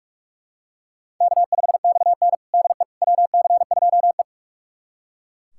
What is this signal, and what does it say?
Morse code, heard as a keyed tone, about 42 wpm, 700 Hz. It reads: K5XN DE WC2E
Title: USA Callsign DE Callsign - 4 Characters - 45wpm